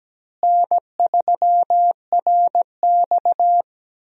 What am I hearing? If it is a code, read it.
Morse code, heard as a keyed tone, about 17 wpm, 700 Hz. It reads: N3RX
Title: USA Call Signs - 4 Characters - Repeated 3x - 17wpm